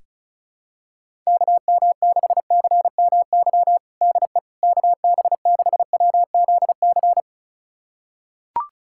Morse code, heard as a keyed tone, about 34 wpm, 700 Hz. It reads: KM6CMY DE KB6WZC
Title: USA Callsign DE Callsign - 6 Characters - 35wpm